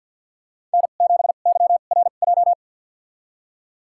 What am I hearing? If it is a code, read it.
Morse code, heard as a keyed tone, about 47 wpm, spaced as for 35 wpm, 690 Hz. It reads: N7YRJ
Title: USA Call Signs - 5 Characters - 2x Character Spacing - 50wpm